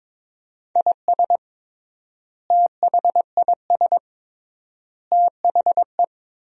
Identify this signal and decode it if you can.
Morse code, heard as a keyed tone, about 21 wpm, 700 Hz. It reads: IS THIS THE
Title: 3x Word-Spacing - Sentences from Top 200 Words - 22wpm